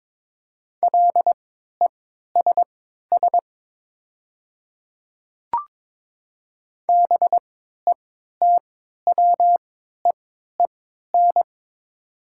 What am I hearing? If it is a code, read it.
Morse code, heard as a keyed tone, about 22 wpm, spaced as for 12 wpm, 700 Hz. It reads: LESS BETWEEN
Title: 3x Character Spacing - Top 500 Words - 22wpm